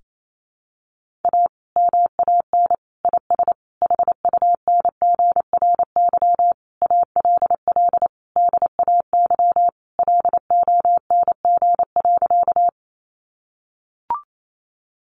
Morse code, heard as a keyed tone, about 28 wpm, 700 Hz. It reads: A MAN IS HUNGRY ALL DAY LONG.
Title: Sentences from Top 1400 Words 28wpm